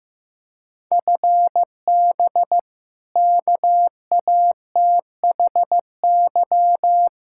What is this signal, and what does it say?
Morse code, heard as a keyed tone, about 15 wpm, 700 Hz. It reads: FB KATHY
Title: Practice QSO - Exchange 3 - More Info - 15wpm